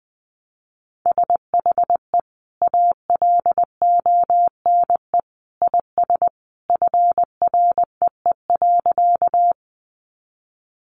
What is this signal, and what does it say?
Morse code, heard as a keyed tone, about 20 wpm, 700 Hz. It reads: SHE ALONE IS FREE.
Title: Sentences from Top 1000 Words 20wpm